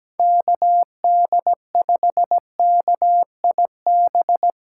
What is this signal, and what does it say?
Morse code, heard as a keyed tone, about 17 wpm, 700 Hz. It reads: KD5KIB
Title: USA Call Signs - 6 Characters - 2x Character Spacing - 17wpm